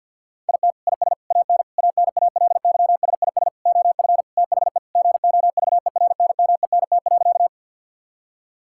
Morse code, heard as a keyed tone, about 50 wpm, 700 Hz. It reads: IT IS AN ANALYSIS OF THE GOVERNMENT.
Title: Sentences from Top 1000 Words - Review Part 5 - 50wpm